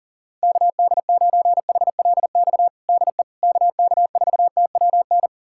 Morse code, heard as a keyed tone, about 37 wpm, 700 Hz. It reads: KD9HLX DE KK4TWN
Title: USA Callsign DE Callsign - 6 Characters - 40wpm